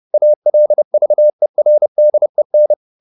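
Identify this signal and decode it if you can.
Morse code, heard as a keyed tone, about 30 wpm, 600 Hz. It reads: ALVERDEN
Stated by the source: High-speed Morse (QRQ) at 30 wpm.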